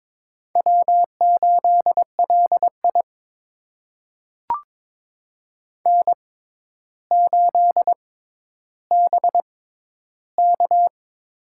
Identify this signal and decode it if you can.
Morse code, heard as a keyed tone, about 22 wpm, spaced as for 11 wpm, 700 Hz. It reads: W8LI N8BK
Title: USA Call Signs - 4 Characters - 6x Character Spacing - 22wpm